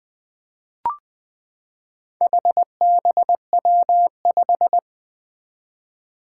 Morse code, heard as a keyed tone, about 20 wpm, 700 Hz. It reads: HBW5
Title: Four Challenging Letter-Number Combinations 20wpm